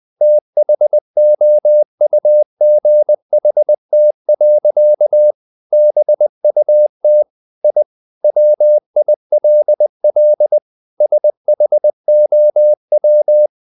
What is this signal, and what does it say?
Morse code, heard as a keyed tone, about 20 wpm, 600 Hz. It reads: THOUGHT. BUT I WILL SHOW
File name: the old man and the sea / the old man and the sea part 1 20wpm